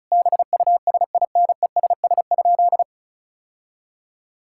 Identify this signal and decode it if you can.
Morse code, heard as a keyed tone, about 36 wpm, 700 Hz. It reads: BUSINESS?
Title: Sentences from Top 1000 Words - Review Part 6 - 35wpm